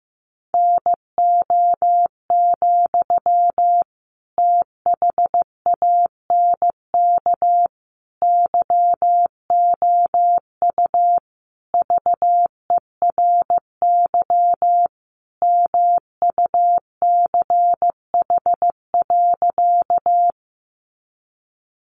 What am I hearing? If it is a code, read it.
Morse code, heard as a keyed tone, about 15 wpm, 700 Hz. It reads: NO, THANK YOU VERY MUCH.
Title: Sentences from Top 600 Words 15wpm